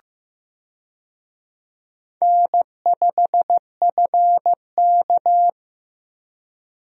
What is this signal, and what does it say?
Morse code, heard as a keyed tone, about 15 wpm, 700 Hz. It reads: N5FK
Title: USA Call Signs - 4 Characters - Repeated 2x - 15wpm